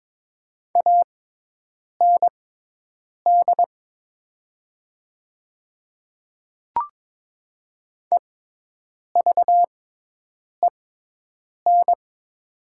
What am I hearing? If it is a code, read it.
Morse code, heard as a keyed tone, about 22 wpm, spaced as for 8 wpm, 690 Hz. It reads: AND EVEN
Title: 6x Character Spacing - Top 200 Words - 22wpm